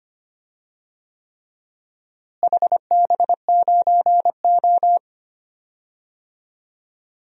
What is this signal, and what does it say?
Morse code, heard as a keyed tone, about 25 wpm, 700 Hz. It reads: HB9O
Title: DXCC-100 Call Signs - 4 Characters - Repeated 3x - 25wpm